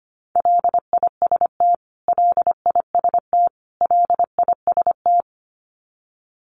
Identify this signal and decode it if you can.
Morse code, heard as a keyed tone, about 25 wpm, 700 Hz. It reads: LIST LIST LIST
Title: Top 500 Words Repeated 25wpm